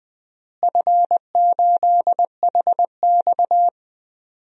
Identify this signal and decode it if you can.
Morse code, heard as a keyed tone, about 20 wpm, 700 Hz. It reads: F8HX
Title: DXCC-100 Call Signs - 4 Characters - Repeated 2x - 20wpm